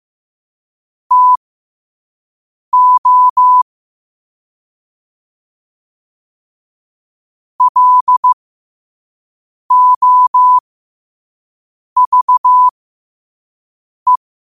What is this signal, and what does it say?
Morse code, heard as a keyed tone, about 15 wpm, spaced as for 5 wpm, 1000 Hz. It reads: TO LOVE
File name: qotdfiles / 5wpm